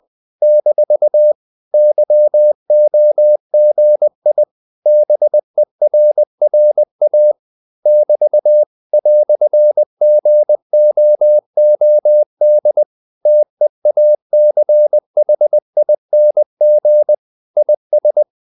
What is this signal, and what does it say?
Morse code, heard as a keyed tone, about 20 wpm, 600 Hz. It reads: - YOGI BERRA <BT> "GOOD TEACHING IS
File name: CW / Practice / EnglishQuotations0320